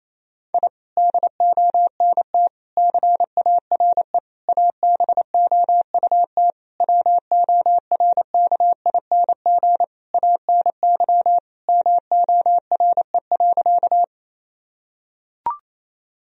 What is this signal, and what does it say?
Morse code, heard as a keyed tone, about 28 wpm, 700 Hz. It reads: I DONT CARE ABOUT WORKING ANY MORE.
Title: Sentences from Top 1000 Words - Review Part 4 - 28wpm